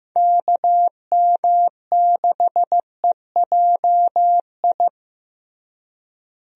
Morse code, heard as a keyed tone, about 15 wpm, 700 Hz. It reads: KM6EJI
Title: USA Call Signs - 6 Characters - Repeated 3x - 15wpm